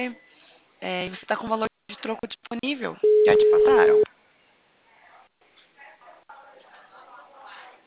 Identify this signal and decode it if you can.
Ringback tone (Europe, 425 Hz), with speech.
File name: out- -2030- - - .27